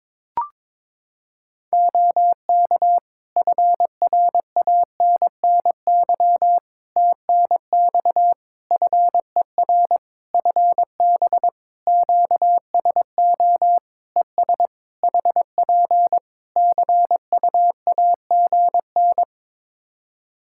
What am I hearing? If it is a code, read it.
Morse code, heard as a keyed tone, about 22 wpm, 700 Hz. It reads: OK FRANNY TNX FER FB QSO ES HP CUAGN
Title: Practice QSO - Exchange 4 - Ending - 22wpm